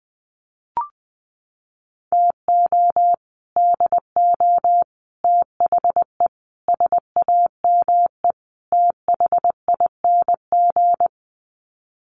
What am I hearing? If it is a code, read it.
Morse code, heard as a keyed tone, about 20 wpm, 700 Hz. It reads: TO DO THE SAME THING
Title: Sets of 5 Words Encore 20wpm